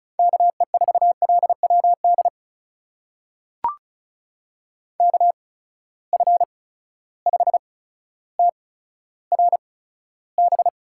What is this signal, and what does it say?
Morse code, heard as a keyed tone, about 36 wpm, spaced as for 15 wpm, 700 Hz. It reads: KE4LWD KF5TRB
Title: USA Call Signs - 6 Characters - 8x Character Spacing - 35wpm